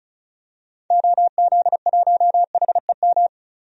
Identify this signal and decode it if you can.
Morse code, heard as a keyed tone, about 36 wpm, 700 Hz. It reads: OZ1HEM
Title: DXCC-100 Call Signs - 6 Characters - 35wpm